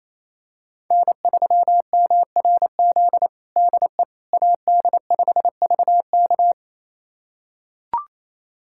Morse code, heard as a keyed tone, about 28 wpm, 700 Hz. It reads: N3MRZ DE AD5VK
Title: USA Callsign DE Callsign - 5 Characters - 28wpm